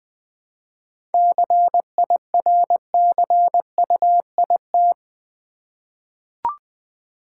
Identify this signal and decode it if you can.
Morse code, heard as a keyed tone, about 20 wpm, 700 Hz. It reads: CIRCUIT
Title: 7 Letter Words - First 3 Letters Given - 20wpm